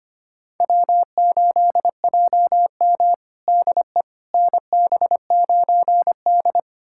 Morse code, heard as a keyed tone, about 25 wpm, 700 Hz. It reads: W8JM DE NB9D
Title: USA Callsign DE Callsign - 4 Characters - 25wpm